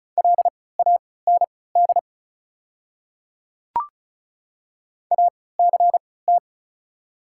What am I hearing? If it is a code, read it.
Morse code, heard as a keyed tone, about 36 wpm, spaced as for 20 wpm, 690 Hz. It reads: LAND ACT
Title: 3x Character Spacing - Top 300 Words - 35wpm